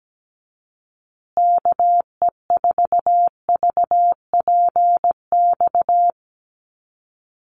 Morse code, heard as a keyed tone, about 17 wpm, 700 Hz. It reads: KE4VPX
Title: USA Call Signs Repeated 2x 17wpm